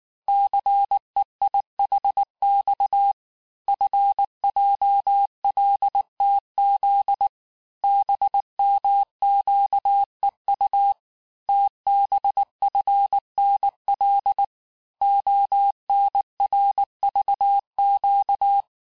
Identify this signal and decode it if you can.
Morse code, heard as a keyed tone, about 19 wpm, 800 Hz. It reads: CEIHX FJLTZ BMQEU TBFNL ONRVQ
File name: cours-du-lundi files / cours-1595-teste-CHIFFRE-vitesse18-MPM